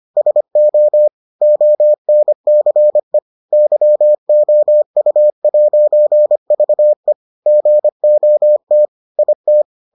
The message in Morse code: SO ONCE YOU'VE GOT IT

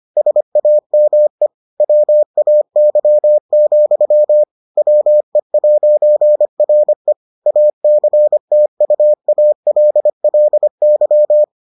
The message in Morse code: SAME WAY, WE'RE ACTUALLY

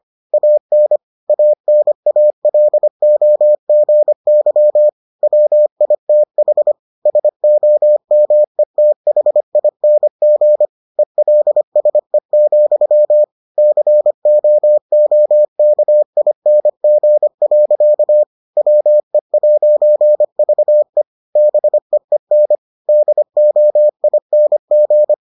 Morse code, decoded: AN ANALOGY WITH SOMETHING ELSE, COOKING. WE'VE BEEN DOING

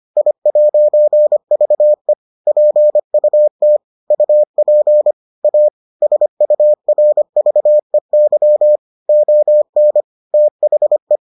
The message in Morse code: I'VE PUT UP A SURVEY ON THE

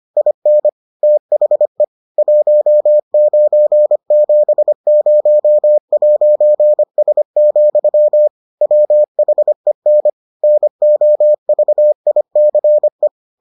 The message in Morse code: IN THE 1970'S, WHEN NOVICE

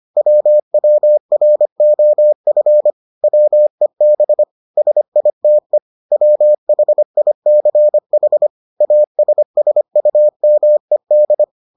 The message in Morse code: WWROF WEB SITE WHICH ASSUMED

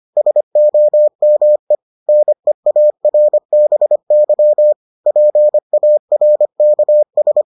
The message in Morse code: SOME NEARBY PARKS

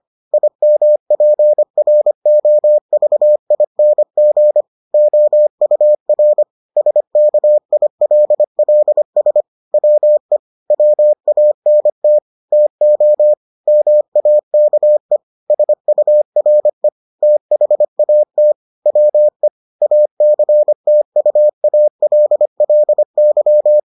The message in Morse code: IMPROVING OUR SKILLS WE WANT TO MAKE SURE THAT WE ACTUALLY

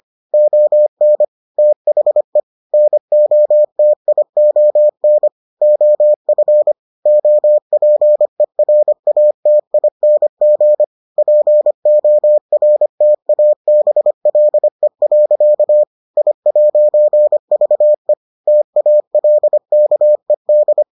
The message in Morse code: ON THE NOTION OF OPERATING PORTABLE. I'VE TALKED

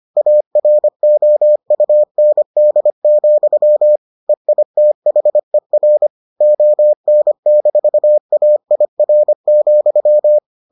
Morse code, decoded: AROUND, EITHER ON-AIR,

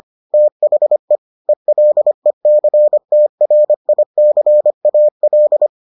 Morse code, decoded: THE ELECTRICAL

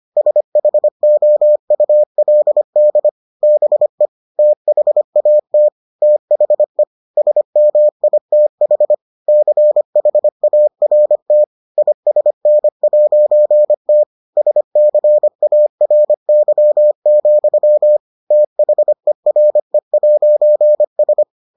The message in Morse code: SHOULD BE THAT THE SMITH CHART ISN'T SCARY, THERE'S

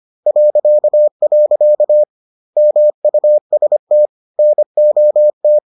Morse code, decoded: .. MUST NOT

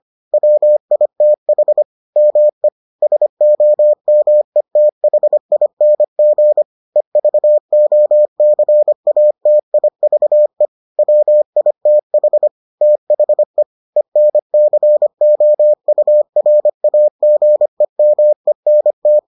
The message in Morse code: WITH ME SOMETHING EVOCATIVE WITH THE ENCOURAGEMENT

